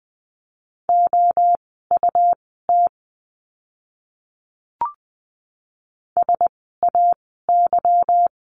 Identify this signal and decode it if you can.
Morse code, heard as a keyed tone, about 20 wpm, spaced as for 14 wpm, 700 Hz. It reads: OUT SAY
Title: 230 Lesson 37f - Letter x - Words 20wpm